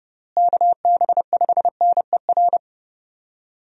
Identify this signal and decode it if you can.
Morse code, heard as a keyed tone, about 30 wpm, 700 Hz. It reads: KB5NER
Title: USA Call Signs - 6 Characters - Repeated 2x - 30wpm